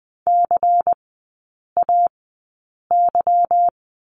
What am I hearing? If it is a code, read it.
Morse code, heard as a keyed tone, about 20 wpm, spaced as for 8 wpm, 690 Hz. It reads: CAY
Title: 159 Lesson 26e - Letter g - Three Characters 20wpm